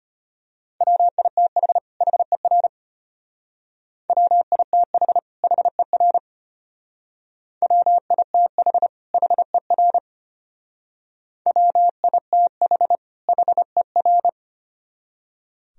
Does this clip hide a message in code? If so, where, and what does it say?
Yes: Morse code from 0.8 to 14.3 s, heard as a keyed tone, about 33 wpm, 700 Hz. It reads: WITH HER WITH HER WITH HER WITH HER